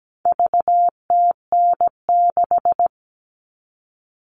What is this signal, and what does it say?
Morse code, heard as a keyed tone, about 17 wpm, 700 Hz. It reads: VTN6
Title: All Cut Numbers - 4-Digit 17wpm